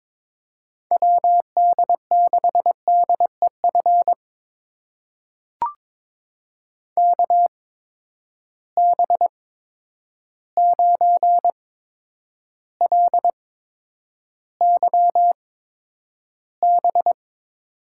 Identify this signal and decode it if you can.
Morse code, heard as a keyed tone, about 22 wpm, spaced as for 9 wpm, 700 Hz. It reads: WD6DEF KB9LYB
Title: USA Call Signs - 6 Characters - 8x Character Spacing - 22wpm